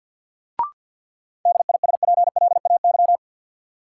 Morse code, heard as a keyed tone, about 48 wpm, 700 Hz. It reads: DISPLAY
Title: Sentences from Top 1200 Words - New Word Drill Alt - 50wpm